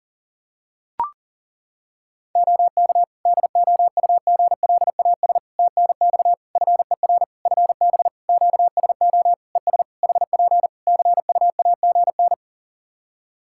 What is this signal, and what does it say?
Morse code, heard as a keyed tone, about 40 wpm, 700 Hz. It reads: OK DOUGLAS TNX FER FB QSO ES HP CUAGN
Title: Practice QSO - Exchange 4 - Ending - 40wpm